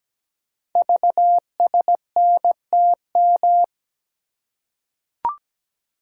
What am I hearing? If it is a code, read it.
Morse code, heard as a keyed tone, about 17 wpm, 700 Hz. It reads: VSNTM
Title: Code Groups - 5 Letter:Numbers - 17wpm